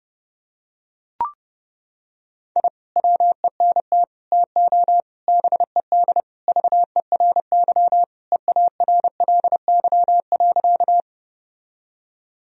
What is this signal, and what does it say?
Morse code, heard as a keyed tone, about 30 wpm, 700 Hz. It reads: I WENT TO BED VERY EARLY.
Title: Sentences from Top 1000 Words - Review Part 2 - 30wpm